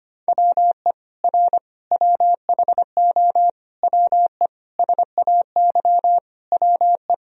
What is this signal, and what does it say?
Morse code, heard as a keyed tone, about 25 wpm, 700 Hz. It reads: WE R WHO WE SAY WE
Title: Sentences from Top 200 Words and Abbreviations 25wpm